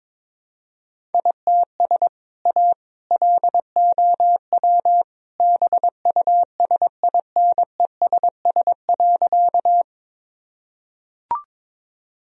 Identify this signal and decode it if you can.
Morse code, heard as a keyed tone, about 22 wpm, 700 Hz. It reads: ITS A LOW BUSINESS.
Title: Sentences from Top 1000 Words - Review Part 1 - 22wpm